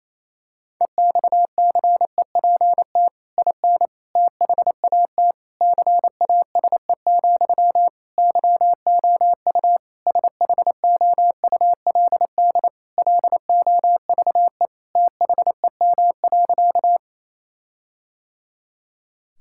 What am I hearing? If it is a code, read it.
Morse code, heard as a keyed tone, about 28 wpm, 700 Hz. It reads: EXCEPT IN THAT CASE, YOU SHOULD LOVE THEM.
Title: Sentences from Top 1600 Words 28wpm